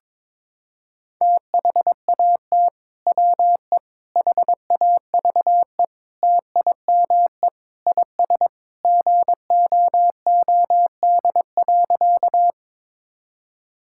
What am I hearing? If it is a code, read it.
Morse code, heard as a keyed tone, about 22 wpm, 700 Hz. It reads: THAT WE HAVE TIME IS GOOD.
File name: practice / 22wpm / Sentences with 3-Word Phrases - 22wpm